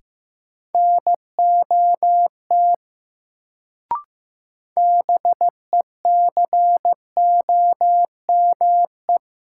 Morse code, heard as a keyed tone, about 15 wpm, 700 Hz. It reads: NOT BECOME
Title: Sentences from Top 600 Words - New Word Drill - 15wpm